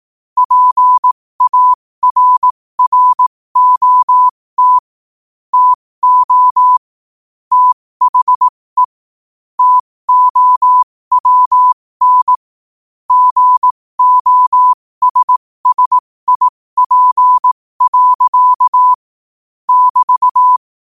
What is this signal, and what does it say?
Morse code, heard as a keyed tone, about 18 wpm, spaced as for 15 wpm, 1000 Hz. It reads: PARROT TO THE TOWN GOSSIP. <BT>
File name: qotdfiles / 15wpm